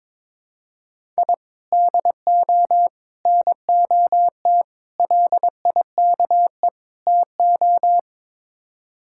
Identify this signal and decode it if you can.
Morse code, heard as a keyed tone, about 22 wpm, 700 Hz. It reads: I DO NOT LIKE TO
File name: practice / 22wpm / Build A Phrase - 5-Word Phrases - 22wpm